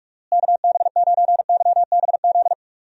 Morse code, heard as a keyed tone, about 46 wpm, 700 Hz. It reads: KB9YBZ
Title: USA Call Signs - 6 Characters - Repeated 3x - 45wpm